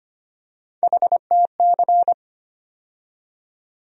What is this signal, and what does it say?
Morse code, heard as a keyed tone, about 25 wpm, 700 Hz. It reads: HTC
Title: USA Call Sign Postfixes - 3 Letters 25wpm